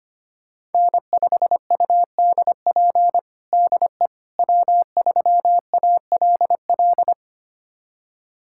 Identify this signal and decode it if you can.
Morse code, heard as a keyed tone, about 25 wpm, 700 Hz. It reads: N5UDP DE W3ALL
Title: USA Callsign DE Callsign - 5 Characters - 25wpm